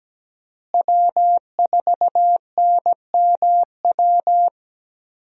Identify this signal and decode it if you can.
Morse code, heard as a keyed tone, about 17 wpm, 700 Hz. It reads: W4NMW
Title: USA Call Signs - 5 Characters - Repeated 3x - 17wpm